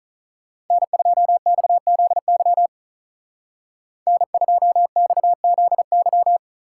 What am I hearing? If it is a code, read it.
Morse code, heard as a keyed tone, about 42 wpm, 700 Hz. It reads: N2XZY N2XZY
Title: Speed Racing - USA 1x3 Call Signs 35wpm